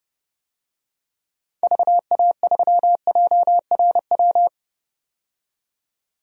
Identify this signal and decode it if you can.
Morse code, heard as a keyed tone, about 31 wpm, 700 Hz. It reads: VA3JRW
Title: DXCC-100 Call Signs - 6 Characters - 30wpm